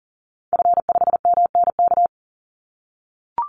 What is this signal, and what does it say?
Morse code, heard as a keyed tone, about 40 wpm, 700 Hz. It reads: F5MNK
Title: Call Signs - 40wpm